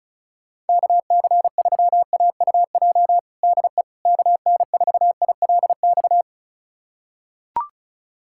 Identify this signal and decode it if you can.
Morse code, heard as a keyed tone, about 36 wpm, 700 Hz. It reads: KC3AUJ DE KN4ILX
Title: USA Callsign DE Callsign - 6 Characters - 35wpm